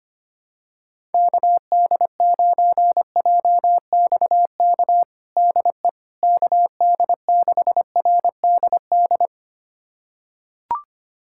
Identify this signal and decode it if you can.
Morse code, heard as a keyed tone, about 25 wpm, 700 Hz. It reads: KD9JXK DE KD6RDD
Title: USA Callsign DE Callsign - 6 Characters - 25wpm